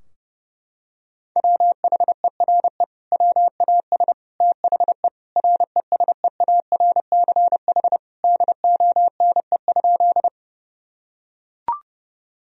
Morse code, heard as a keyed tone, about 30 wpm, 700 Hz. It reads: WHERE WAS THE RESEARCH DONE?
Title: Sentences from Top 1000 Words - Review Part 4 - 30wpm